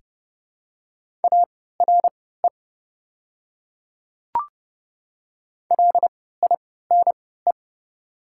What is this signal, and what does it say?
Morse code, heard as a keyed tone, about 30 wpm, spaced as for 17 wpm, 700 Hz. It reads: ARE LINE
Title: 3x Character Spacing - Top 300 Words - 30wpm